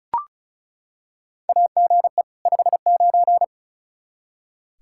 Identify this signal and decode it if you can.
Morse code, heard as a keyed tone, about 34 wpm, 700 Hz. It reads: AGE 59
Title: Practice QSO - Exchange 3 - More Info - 35wpm